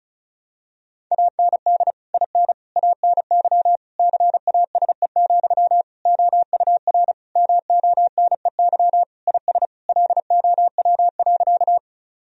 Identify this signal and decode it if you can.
Morse code, heard as a keyed tone, about 34 wpm, 700 Hz. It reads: AND IN ANY CASE, OUR MONEY IS LOW.
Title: Sentences from Top 1000 Words - Review Part 1 - 35wpm